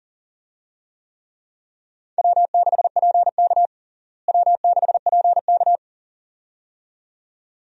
Morse code, heard as a keyed tone, about 40 wpm, 700 Hz. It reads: W6PK W6PK
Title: USA Call Signs - 4 Characters - Repeated 2x - 40wpm